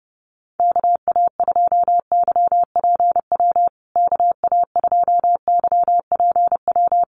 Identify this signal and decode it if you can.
Morse code, heard as a keyed tone, about 30 wpm, 700 Hz. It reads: KA2YPW KA2YPW
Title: USA Call Signs Repeated 3x 30wpm